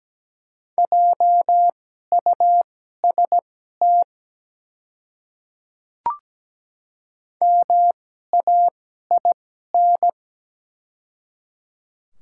Morse code, heard as a keyed tone, about 17 wpm, spaced as for 12 wpm, 690 Hz. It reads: JUST MAIN
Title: 2x Character Spacing - Top 300 Words - 17wpm